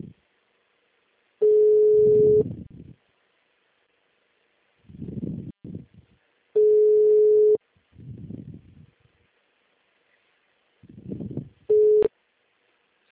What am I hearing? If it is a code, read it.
Ringback tone (Europe, 425 Hz).
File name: out- -2033- - - .30010